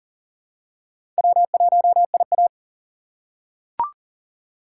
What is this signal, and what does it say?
Morse code, heard as a keyed tone, about 40 wpm, 700 Hz. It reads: W1IA
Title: USA Call Signs - 4 Characters - 2x Character Spacing - 40wpm